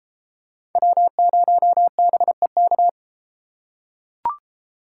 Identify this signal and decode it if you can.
Morse code, heard as a keyed tone, about 33 wpm, 700 Hz. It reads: W0BEK